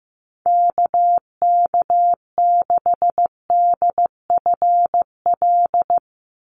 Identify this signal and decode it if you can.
Morse code, heard as a keyed tone, about 15 wpm, 700 Hz. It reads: KK6DFL
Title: USA Call Signs 15wpm